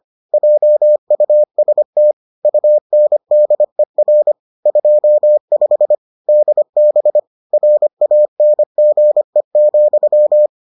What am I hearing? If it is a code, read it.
Morse code, heard as a keyed tone, about 25 wpm, 600 Hz. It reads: JUST UNDER 25 DB RANGE,